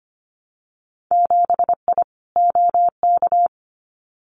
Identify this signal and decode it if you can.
Morse code, heard as a keyed tone, about 25 wpm, 700 Hz. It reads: 7I OK
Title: Winter Field Day Exchanges Encore 25wpm